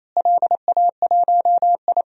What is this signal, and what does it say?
Morse code, heard as a keyed tone, about 27 wpm, 700 Hz. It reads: LA1I
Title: DXCC-100 Call Signs - 4 Characters - Repeated 3x - 28wpm